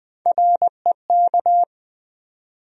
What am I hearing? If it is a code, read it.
Morse code, heard as a keyed tone, about 20 wpm, 700 Hz. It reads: REK
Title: USA Call Sign Postfixes - 3 Letters 20wpm